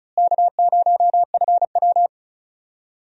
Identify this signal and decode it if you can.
Morse code, heard as a keyed tone, about 36 wpm, 700 Hz. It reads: K0FW
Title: USA Call Signs - 4 Characters - Repeated 3x - 35wpm